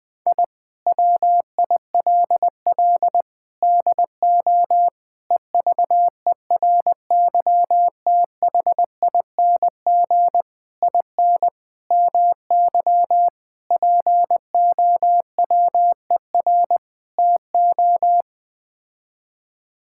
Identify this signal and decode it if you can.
Morse code, heard as a keyed tone, about 20 wpm, 700 Hz. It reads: I WILL DO EVERYTHING IN MY POWER TO
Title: Sets of 8 Words - 20wpm